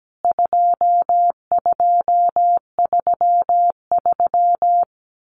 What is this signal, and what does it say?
Morse code, heard as a keyed tone, about 17 wpm, 700 Hz. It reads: 2233
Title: CWT Exchange 17wpm